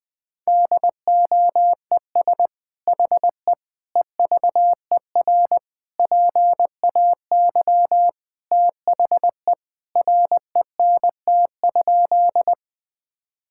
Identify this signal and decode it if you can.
Morse code, heard as a keyed tone, about 20 wpm, 700 Hz. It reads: DOES HE EVER PAY THE RENT?
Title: Sentences from Top 1000 Words - Review Part 5 - 20wpm